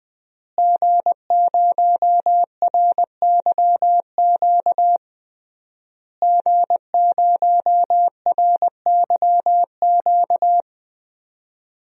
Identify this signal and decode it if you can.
Morse code, heard as a keyed tone, about 21 wpm, 700 Hz. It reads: G0RYQ G0RYQ
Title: DXCC-100 Call Signs - 5 Characters - Repeated 3x - 20wpm